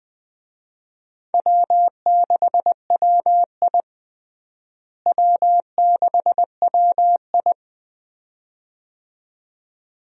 Morse code, heard as a keyed tone, about 20 wpm, 700 Hz. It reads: W6WI W6WI
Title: USA Call Signs - 4 Characters - Repeated 2x - 20wpm